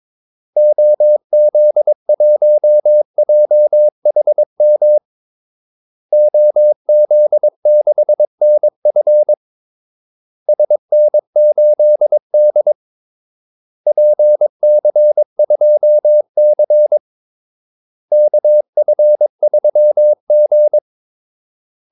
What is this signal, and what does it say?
Morse code, heard as a keyed tone, about 22 wpm, 600 Hz. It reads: OZ1JHM OZ6NF SN8D PC2C KF3G
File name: Callsigns-22wpm